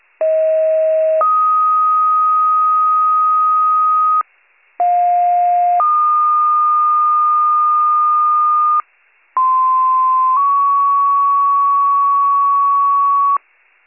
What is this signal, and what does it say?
Two-tone page, heard as a one-second tone, then a three-second tone, about 640 then 1250 Hz, about 710 then 1220 Hz, about 1040 then 1120 Hz. It is heard from 0.2 to 13.4 s.